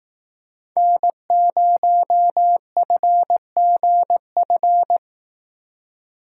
Morse code, heard as a keyed tone, about 19 wpm, 700 Hz. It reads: N0FGF
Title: Speed Racing - USA 1x3 Call Signs 15wpm